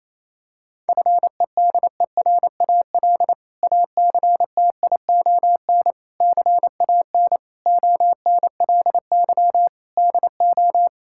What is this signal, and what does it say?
Morse code, heard as a keyed tone, about 28 wpm, 700 Hz. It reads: FEDERAL ACTION CAN ONLY DO
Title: Sentences from Top 1000 Words - Review Part 5 - 28wpm